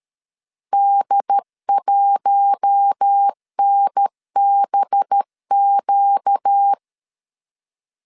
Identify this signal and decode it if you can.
Morse code, heard as a keyed tone, about 13 wpm, 800 Hz. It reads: D1NBQ
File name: CW / DET / 1 6 1500C S B 12wpm 3431